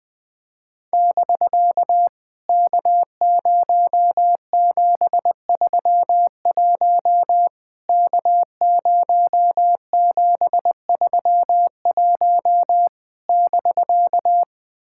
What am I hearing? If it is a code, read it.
Morse code, heard as a keyed tone, about 20 wpm, 700 Hz. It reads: <BK> K0731 K0731 <BK>
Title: POTA QSO Practice - 20wpm